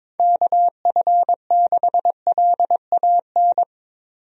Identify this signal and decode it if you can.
Morse code, heard as a keyed tone, about 22 wpm, 700 Hz. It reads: KF6LAN
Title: USA Call Signs - 6 Characters - 22wpm